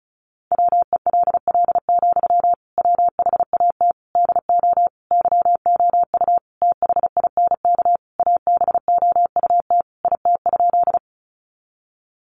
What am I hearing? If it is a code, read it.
Morse code, heard as a keyed tone, about 34 wpm, 700 Hz. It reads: WELL, WHAT DO YOU THINK ABOUT IT?
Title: Sentences from Top 200 Words 35wpm